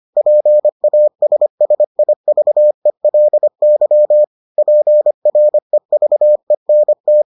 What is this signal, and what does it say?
Morse code, heard as a keyed tone, about 25 wpm, 600 Hz. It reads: PASSIVELY PREVENT